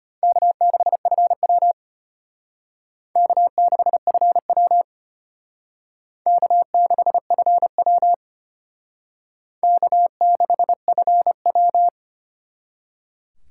Morse code, heard as a keyed tone, about 32 wpm, 700 Hz. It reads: K6FW K6FW K6FW K6FW